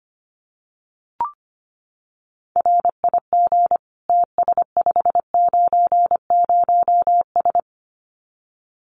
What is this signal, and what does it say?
Morse code, heard as a keyed tone, about 25 wpm, 700 Hz. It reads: RIG TS590S
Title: Common QSO Phrases Encore 25wpm